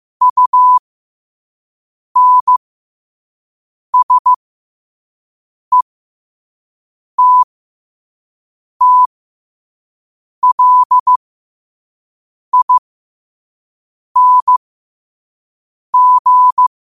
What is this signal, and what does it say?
Morse code, heard as a keyed tone, about 15 wpm, spaced as for 5 wpm, 1000 Hz. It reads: UNSETTLING
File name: qotdfiles / 5wpm